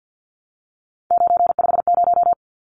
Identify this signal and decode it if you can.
Morse code, heard as a keyed tone, about 50 wpm, 700 Hz. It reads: 950
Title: Numbers - 3-Digit 50wpm